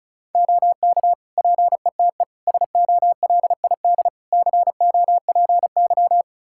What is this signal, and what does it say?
Morse code, heard as a keyed tone, about 34 wpm, 700 Hz. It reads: OK PETE SOLID COPY
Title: Practice QSO - Exchange 3 - More Info - 35wpm